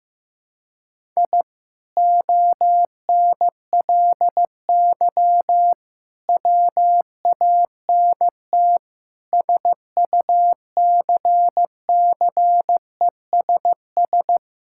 Morse code, heard as a keyed tone, about 15 wpm, 700 Hz. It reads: I ONLY WANT SUCCESS
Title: Sentences from Top 1000 Words - Review Part 6 - 15wpm